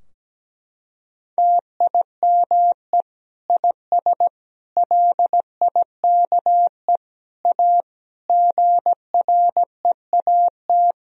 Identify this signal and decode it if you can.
Morse code, heard as a keyed tone, about 17 wpm, 700 Hz. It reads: TIME IS LIKE A GREAT
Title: Sentences from Top 1000 Words - Review Part 3 - 17wpm